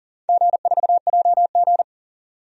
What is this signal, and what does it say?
Morse code, heard as a keyed tone, about 40 wpm, 700 Hz. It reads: G4JG
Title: DXCC-100 Call Signs - 4 Characters - 40wpm